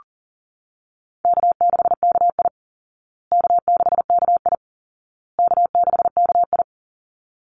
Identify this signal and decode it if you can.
Morse code, heard as a keyed tone, about 40 wpm, 700 Hz. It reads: K6KI K6KI K6KI
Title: Easy Call Signs Repeated 3x - 40wpm